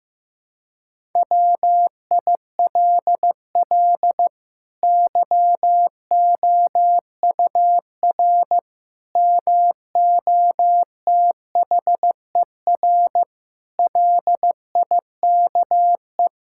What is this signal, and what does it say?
Morse code, heard as a keyed tone, about 15 wpm, 700 Hz. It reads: WILL YOUR MOTHER LIKE
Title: Sentences from Top 1000 Words - Review Part 5 - 15wpm